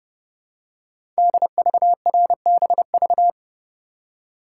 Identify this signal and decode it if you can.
Morse code, heard as a keyed tone, about 30 wpm, 700 Hz. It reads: DVRBV
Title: Code Groups - 5 Letters - 30wpm